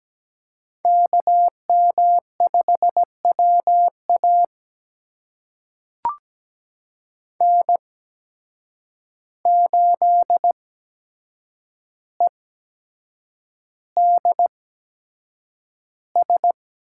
Morse code, heard as a keyed tone, about 17 wpm, spaced as for 7 wpm, 700 Hz. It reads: KM5WA N8EDS
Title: USA Call Signs - 5 Characters - 8x Character Spacing - 17wpm